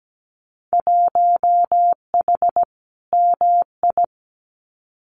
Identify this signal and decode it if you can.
Morse code, heard as a keyed tone, about 17 wpm, 700 Hz. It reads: 1H MI
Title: Winter Field Day Exchanges 17wpm